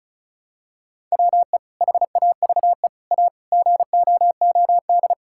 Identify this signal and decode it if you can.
Morse code, heard as a keyed tone, about 36 wpm, 700 Hz. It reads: WE HAVE A GOOD
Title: Sentences from Top 1000 Words - Review Part 5 - 35wpm